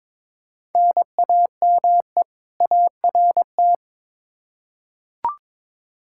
Morse code, heard as a keyed tone, about 21 wpm, 700 Hz. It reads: NAME ART
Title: K1USN SST QSO Practice 22wpm